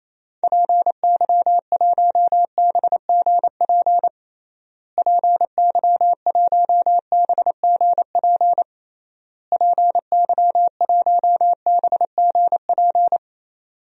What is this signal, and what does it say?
Morse code, heard as a keyed tone, about 28 wpm, 700 Hz. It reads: PY1BGP PY1BGP PY1BGP
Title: DXCC-100 Call Signs - 6 Characters - Repeated 3x - 28wpm